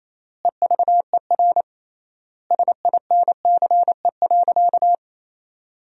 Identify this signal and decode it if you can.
Morse code, heard as a keyed tone, about 28 wpm, 700 Hz. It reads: EVER SINCE.
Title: 3x Word-Spacing - Sentences from Top 400 Words - 28wpm